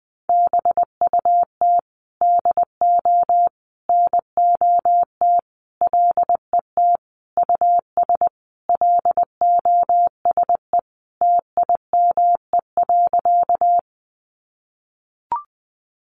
Morse code, heard as a keyed tone, about 20 wpm, 700 Hz. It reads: BUT DO NOT LET US LOSE TIME.
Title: Sentences from Top 1100 Words 20wpm